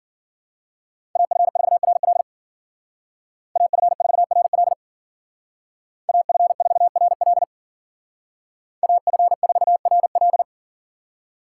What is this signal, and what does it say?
Morse code, heard as a keyed tone, about 49 wpm, 700 Hz. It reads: AF4RL AF4RL AF4RL AF4RL
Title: Speed Racing - USA 2x2 Call Signs 40wpm